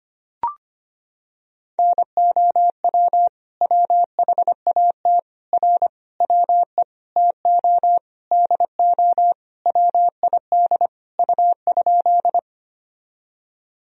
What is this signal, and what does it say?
Morse code, heard as a keyed tone, about 25 wpm, 700 Hz. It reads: NOW WHAT R WE TO DO WID U?
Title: Sentences from Top 100 Words and Abbreviations 25wpm